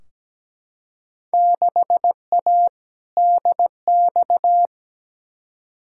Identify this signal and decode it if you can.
Morse code, heard as a keyed tone, about 17 wpm, 700 Hz. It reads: 6A DX
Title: Field Day Exchanges Encore - 17wpm